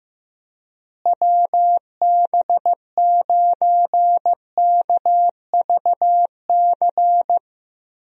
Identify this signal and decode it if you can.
Morse code, heard as a keyed tone, about 15 wpm, 700 Hz. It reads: WB9KVC
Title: USA Call Signs - 6 Characters - Repeated 3x - 15wpm